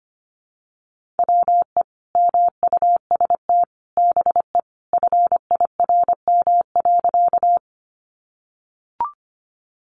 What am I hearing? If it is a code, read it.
Morse code, heard as a keyed tone, about 25 wpm, 700 Hz. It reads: WE MUST BE FIRM.
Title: Sentences from Top 1400 Words 25wpm